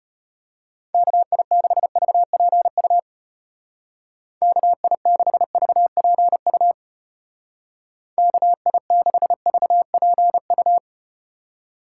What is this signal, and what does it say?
Morse code, heard as a keyed tone, about 34 wpm, 700 Hz. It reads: KI6VPU KI6VPU KI6VPU